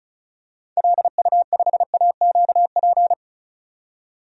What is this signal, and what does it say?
Morse code, heard as a keyed tone, about 36 wpm, 700 Hz. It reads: LU5AQP
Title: DXCC-100 Call Signs - 6 Characters - 35wpm